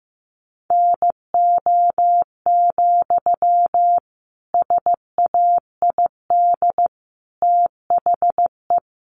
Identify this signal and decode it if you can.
Morse code, heard as a keyed tone, about 15 wpm, 700 Hz. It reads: NO, SAID THE
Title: Sentences from Top 300 Words 15wpm